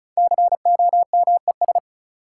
Morse code, heard as a keyed tone, about 36 wpm, 700 Hz. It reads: COMES
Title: Sentences from Top 1000 Words - Review Part 5 - New Word Drill - 35wpm